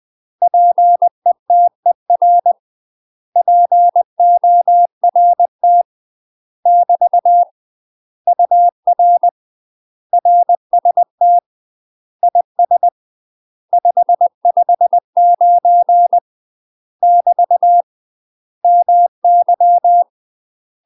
Morse code, heard as a keyed tone, about 20 wpm, 700 Hz. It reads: PETER PORT <BT> UR RST IS 559 <BT> MY
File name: UK MASTER QSOs TRAINING / 20WPM / UK MASTER QSOs 0007